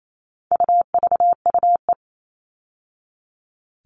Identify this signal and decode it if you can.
Morse code, heard as a keyed tone, about 28 wpm, 700 Hz. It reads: UVUE
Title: All Cut Numbers - 4-Digit 28wpm